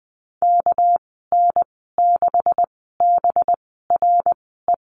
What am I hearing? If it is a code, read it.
Morse code, heard as a keyed tone, about 20 wpm, spaced as for 15 wpm, 700 Hz. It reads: KN6BRE
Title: 207 Lesson 33g - Letter j - Call Signs 20wpm